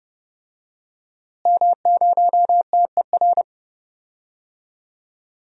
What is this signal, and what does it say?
Morse code, heard as a keyed tone, about 30 wpm, 700 Hz. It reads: M0TER
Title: DXCC-100 Call Signs - 5 Characters - Repeated 2x - 30wpm